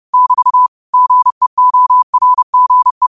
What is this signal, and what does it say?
Morse code, heard as a keyed tone, about 30 wpm, 1000 Hz. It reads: <BT> GEORGE
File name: qotdfiles / 30wpm